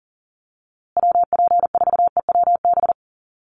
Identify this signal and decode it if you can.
Morse code, heard as a keyed tone, about 40 wpm, 700 Hz. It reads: WP4EWB
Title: USA Call Signs Repeated 2x 40wpm